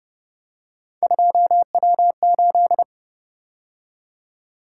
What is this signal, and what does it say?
Morse code, heard as a keyed tone, about 30 wpm, 700 Hz. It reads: 2W8
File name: Speed-Racing - Three Letter-Number Combinations - 25wpm